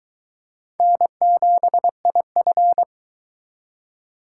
Morse code, heard as a keyed tone, about 21 wpm, 700 Hz. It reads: N7IF